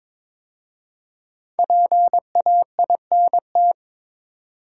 Morse code, heard as a keyed tone, about 22 wpm, 700 Hz. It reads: PAINT
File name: 5 Letter Words - First 3 Letters Given - 22wpm